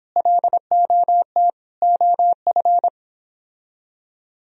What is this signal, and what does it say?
Morse code, heard as a keyed tone, about 29 wpm, 700 Hz. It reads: LOT OF